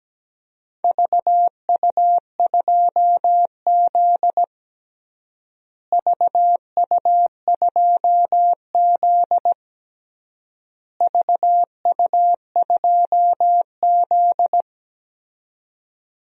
Morse code, heard as a keyed tone, about 17 wpm, 700 Hz. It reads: VU2Z VU2Z VU2Z
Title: DXCC-100 Call Signs - 4 Characters - Repeated 3x - 17wpm